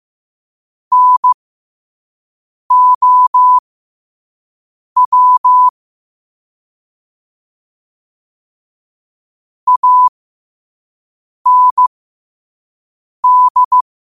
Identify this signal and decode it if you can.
Morse code, heard as a keyed tone, about 15 wpm, spaced as for 5 wpm, 1000 Hz. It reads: NOW AND
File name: qotdfiles / 5wpm